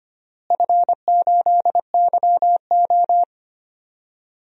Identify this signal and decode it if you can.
Morse code, heard as a keyed tone, about 25 wpm, 700 Hz. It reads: F8YO
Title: DXCC-100 Call Signs - 4 Characters - Repeated 3x - 25wpm